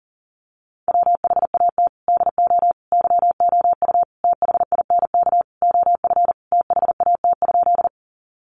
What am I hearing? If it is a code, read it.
Morse code, heard as a keyed tone, about 40 wpm, 700 Hz. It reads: WHAT DO YOU THINK OF THAT?
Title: Sentences from Top 200 Words 40wpm